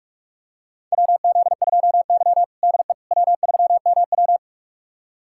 Morse code, heard as a keyed tone, about 41 wpm, 700 Hz. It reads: WZ2Y DE W3MW
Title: USA Callsign DE Callsign - 4 Characters - 45wpm